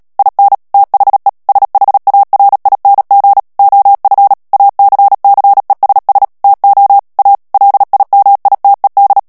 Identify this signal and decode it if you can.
Morse code, heard as a keyed tone, about 37 wpm, 800 Hz. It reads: IN THE SHARING OF ACCESS TO A LIMITED
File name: morse mp3 / ANGLAIS texte clair 3 vitesse 35 mots minute